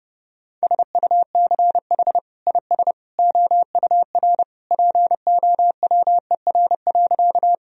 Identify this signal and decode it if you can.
Morse code, heard as a keyed tone, about 30 wpm, 700 Hz. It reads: SUCH IS OUR POWER.
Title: Sentences from Top 1000 Words - Review Part 2 - 30wpm